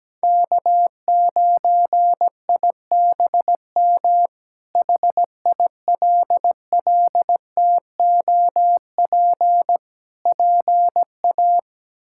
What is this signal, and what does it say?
Morse code, heard as a keyed tone, about 17 wpm, 700 Hz. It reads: K9IBM HILLTOP PA
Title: K1USN SST QSO Practice - 17wpm